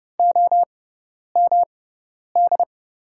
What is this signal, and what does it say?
Morse code, heard as a keyed tone, about 30 wpm, spaced as for 10 wpm, 700 Hz. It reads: OMD
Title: USA Call Signs - 6 Characters - 6x Character Spacing - 30wpm